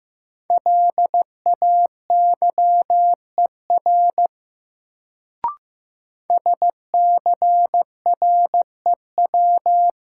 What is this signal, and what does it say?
Morse code, heard as a keyed tone, about 15 wpm, 700 Hz. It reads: LAYER SCREW
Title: Sentences from Top 1600 Words - New Word Drill Alt - 15wpm